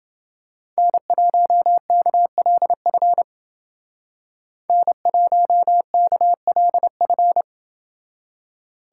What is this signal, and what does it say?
Morse code, heard as a keyed tone, about 29 wpm, 700 Hz. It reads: N1KLF N1KLF